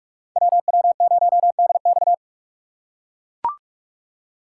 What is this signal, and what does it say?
Morse code, heard as a keyed tone, about 40 wpm, 700 Hz. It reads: WW0DX